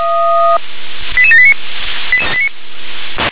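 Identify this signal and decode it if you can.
Five-tone selective call (ZVEI), heard as quick runs of five tones, tones of about 70 ms. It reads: 88789
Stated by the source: ZVEI five-tone selective call received on an SDR.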